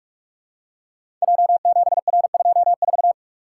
Morse code, heard as a keyed tone, about 44 wpm, 700 Hz. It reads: J7R24
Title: Code Groups - 5 Letter:Numbers - 45wpm